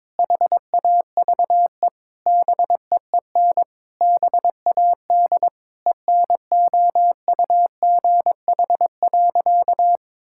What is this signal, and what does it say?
Morse code, heard as a keyed tone, about 22 wpm, 700 Hz. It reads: HAVE BEEN BAD ENOUGH.
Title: Sentences from Top 1000 Words - Review Part 5 - 22wpm